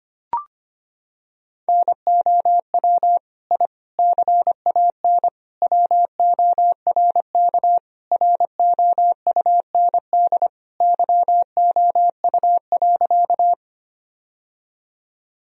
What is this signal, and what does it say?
Morse code, heard as a keyed tone, about 25 wpm, 700 Hz. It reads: NOW I CAN WORK ROUND YOU.
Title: Sentences from Top 1000 Words - Review Part 1 - 25wpm